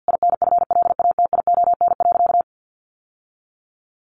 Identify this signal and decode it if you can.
Morse code, heard as a keyed tone, about 50 wpm, 700 Hz. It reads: INFLATION.
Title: Sentences from Top 1100 Words 50wpm